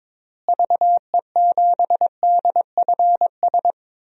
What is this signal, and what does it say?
Morse code, heard as a keyed tone, about 22 wpm, 700 Hz. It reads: VE7DFS
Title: DXCC-100 Call Signs - 6 Characters - Repeated 3x - 22wpm